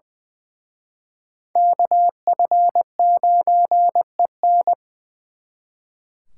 Morse code, heard as a keyed tone, about 24 wpm, 700 Hz. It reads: KF9EN
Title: Speed Racing - USA Call Signs 20wpm